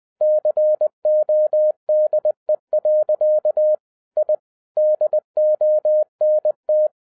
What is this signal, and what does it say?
Morse code, heard as a keyed tone, about 20 wpm, 610 Hz. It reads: CODE. I DONT